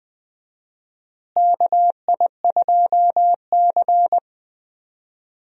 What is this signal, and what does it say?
Morse code, heard as a keyed tone, about 20 wpm, 700 Hz. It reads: KI2C
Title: USA Call Signs - 4 Characters - Repeated 2x - 20wpm